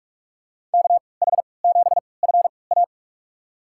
Morse code, heard as a keyed tone, about 46 wpm, spaced as for 25 wpm, 700 Hz. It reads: KH7FA
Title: USA Call Signs - 5 Characters - 3x Character Spacing - 45wpm